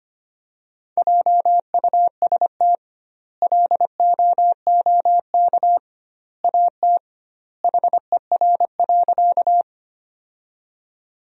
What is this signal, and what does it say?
Morse code, heard as a keyed tone, about 25 wpm, 700 Hz. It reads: JUST LOOK AT HER.
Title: 2x Word-Spacing - Sentences from Top 200 Words - 25wpm